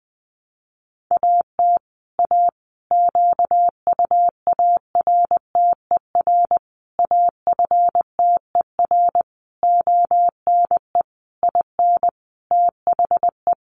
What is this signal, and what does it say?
Morse code, heard as a keyed tone, about 20 wpm, 700 Hz. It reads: AT A QUARTER AFTER ONE IN THE
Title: Sentences from Top 1300 Words 20wpm